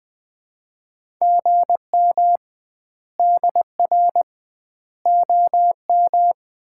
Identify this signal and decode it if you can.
Morse code, heard as a keyed tone, about 20 wpm, 700 Hz. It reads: GM DR OM
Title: Common QSO Phrases - 2x Word-Spacing - 20wpm